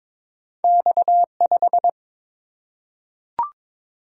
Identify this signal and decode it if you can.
Morse code, heard as a keyed tone, about 22 wpm, 700 Hz. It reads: X5
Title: Single-Letter Single-Number 22wpm